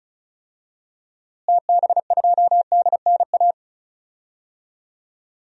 Morse code, heard as a keyed tone, about 36 wpm, 700 Hz. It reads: TB2DNA
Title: DXCC-100 Call Signs - 6 Characters - Repeated 2x - 35wpm